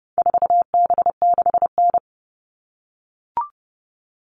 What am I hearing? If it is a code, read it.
Morse code, heard as a keyed tone, about 30 wpm, 700 Hz. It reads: 4B6N
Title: All Cut Numbers - 4-Digit 30wpm